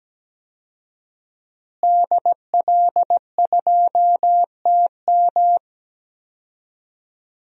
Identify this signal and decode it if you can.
Morse code, heard as a keyed tone, about 17 wpm, 700 Hz. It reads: DL2TM
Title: DXCC-100 Call Signs - 5 Characters - Repeated 3x - 17wpm